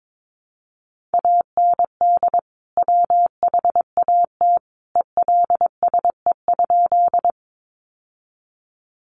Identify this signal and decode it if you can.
Morse code, heard as a keyed tone, about 22 wpm, 700 Hz. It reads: AND WHAT ELSE?
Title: Sentences from Top 800 Words 22wpm